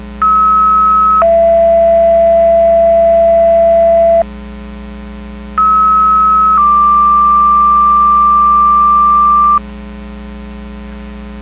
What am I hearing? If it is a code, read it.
Two-tone page, heard as a one-second tone, then a three-second tone, about 1290 then 670 Hz, about 1290 then 1180 Hz. It is heard from 0.2 to 9.6 s.
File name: Portage 154.385 Paging